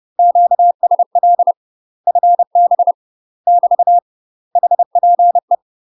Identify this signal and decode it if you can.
Morse code, heard as a keyed tone, about 30 wpm, 700 Hz. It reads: QSL FB <BT> HPE
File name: UK MASTER QSOs TRAINING / 30WPM / UK MASTER QSOs 0039